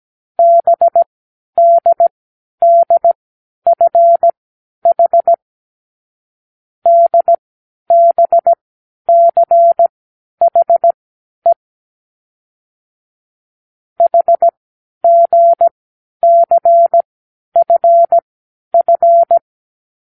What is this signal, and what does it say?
Morse code, heard as a keyed tone, about 17 wpm, spaced as for 10 wpm, 680 Hz. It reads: BDDFH DBCHE HGCFF
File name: k7qo / Lesson12(A-H)